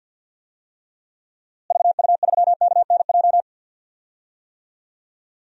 Morse code, heard as a keyed tone, about 47 wpm, 700 Hz. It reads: VU3KNJ
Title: DXCC-100 Call Signs - 6 Characters - Repeated 2x - 50wpm